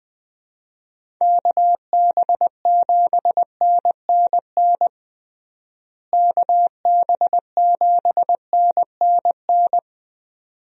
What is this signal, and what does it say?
Morse code, heard as a keyed tone, about 20 wpm, 700 Hz. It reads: KB7NNN KB7NNN
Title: USA Call Signs - 6 Characters - Repeated 2x - 20wpm